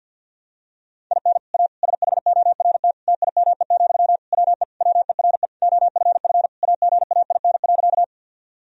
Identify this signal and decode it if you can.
Morse code, heard as a keyed tone, about 48 wpm, 700 Hz. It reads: IN A SHORT TIME, WE WERE OFF AGAIN.
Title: Sentences from Top 1000 Words - Review Part 2 - 50wpm